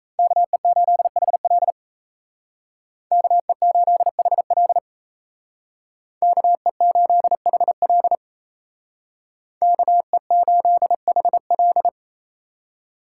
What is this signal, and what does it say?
Morse code, heard as a keyed tone, about 36 wpm, 700 Hz. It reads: KE8HL KE8HL KE8HL KE8HL